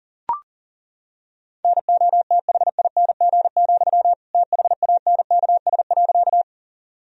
Morse code, heard as a keyed tone, about 40 wpm, 700 Hz. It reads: NOTHING, THANKS.
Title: Sentences from Top 1000 Words - Review Part 3 - 40wpm